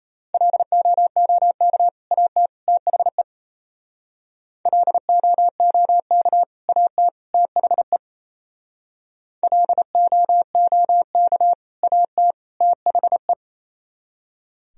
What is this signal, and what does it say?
Morse code, heard as a keyed tone, about 34 wpm, 700 Hz. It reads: LOOK AT THE LOOK AT THE LOOK AT THE
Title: Speed-Racing - Sets of 3 Words 28wpm